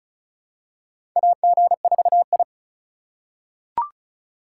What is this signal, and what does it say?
Morse code, heard as a keyed tone, about 36 wpm, 700 Hz. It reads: AG4I
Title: USA Call Signs - 4 Characters - 4x Character Spacing - 35wpm